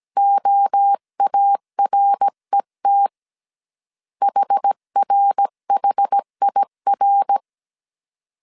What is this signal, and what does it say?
Morse code, heard as a keyed tone, about 17 wpm, 800 Hz. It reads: OARET HRHIR
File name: CW / DET / 1 3 1500C S A 16wpm 2141